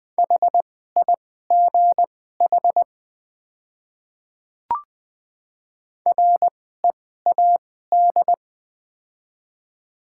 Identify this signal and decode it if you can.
Morse code, heard as a keyed tone, about 20 wpm, spaced as for 14 wpm, 700 Hz. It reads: HIGH READ
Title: 2x Character Spacing - Top 300 Words - 20wpm